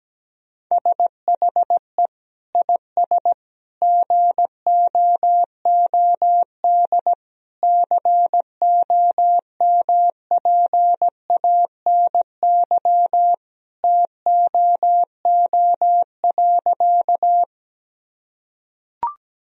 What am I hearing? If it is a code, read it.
Morse code, heard as a keyed tone, about 17 wpm, 700 Hz. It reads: SHE IS GOOD COMPANY TOO.
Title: Sentences from Top 1000 Words - Review Part 3 - 17wpm